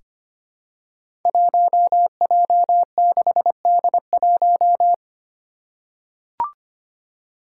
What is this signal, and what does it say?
Morse code, heard as a keyed tone, about 25 wpm, 700 Hz. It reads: 1J6D1
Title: Five Challenging Letter-Number Combinations 25wpm